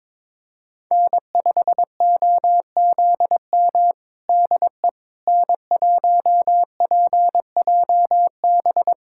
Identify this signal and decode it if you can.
Morse code, heard as a keyed tone, about 22 wpm, 700 Hz. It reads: N5OZM DE N1PJB
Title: USA Callsign DE Callsign - 5 Characters - 22wpm